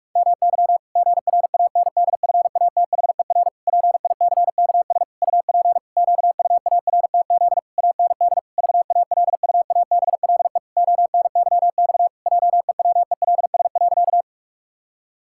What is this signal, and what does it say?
Morse code, heard as a keyed tone, about 46 wpm, 700 Hz. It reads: MY GRANDFATHER PICKS UP QUARTZ AND VALUABLE ONYX JEWELS.
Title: Pangram Challenge v2 - 45wpm